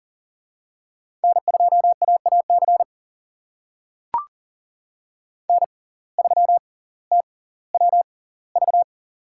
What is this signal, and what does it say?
Morse code, heard as a keyed tone, about 40 wpm, spaced as for 20 wpm, 700 Hz. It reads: N2AAC N3TWV
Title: USA Call Signs - 5 Characters - 6x Character Spacing - 40wpm